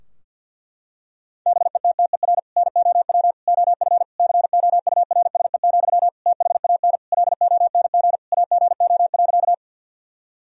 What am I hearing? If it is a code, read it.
Morse code, heard as a keyed tone, about 51 wpm, 700 Hz. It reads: BETTER NOW OF COURSE, THAN LONG AGO.
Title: Sentences from Top 1000 Words - Review Part 3 - 50wpm